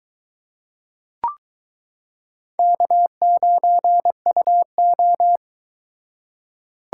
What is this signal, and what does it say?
Morse code, heard as a keyed tone, about 22 wpm, 700 Hz. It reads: K9UO